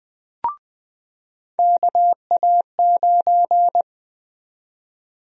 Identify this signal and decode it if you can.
Morse code, heard as a keyed tone, about 20 wpm, 700 Hz. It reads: KA9
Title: USA Call Sign Prefixes - 20wpm